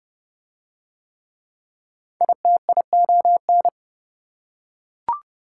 Morse code, heard as a keyed tone, about 27 wpm, 700 Hz. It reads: ITION